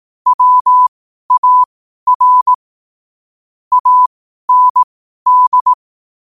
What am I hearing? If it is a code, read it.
Morse code, heard as a keyed tone, about 18 wpm, spaced as for 12 wpm, 1000 Hz. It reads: WAR AND
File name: qotdfiles / 12wpm